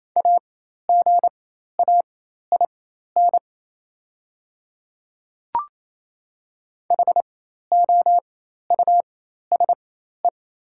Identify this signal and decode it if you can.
Morse code, heard as a keyed tone, about 28 wpm, spaced as for 13 wpm, 700 Hz. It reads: AGAIN HOUSE
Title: 4x Character Spacing - Top 200 Words - 28wpm